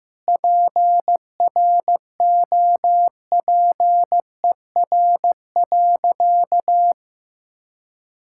Morse code, heard as a keyed tone, about 15 wpm, 700 Hz. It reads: PROPER.
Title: Sentences from Top 1000 Words - Review Part 5 - 15wpm